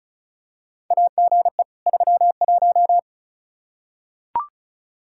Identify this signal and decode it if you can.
Morse code, heard as a keyed tone, about 36 wpm, 700 Hz. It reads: AGE 31
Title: Practice QSO - Exchange 3 - More Info - 35wpm